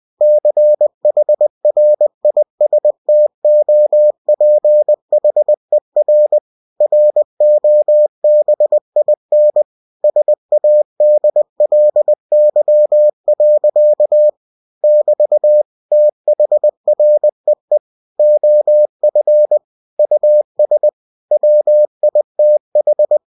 Morse code, heard as a keyed tone, about 20 wpm, 600 Hz. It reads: CHRISTOPHER ROBIN SADLY. <BT> THREE OF US WITH
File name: cw-books / winnie-the-pooh / winnie-20wpm-page-0099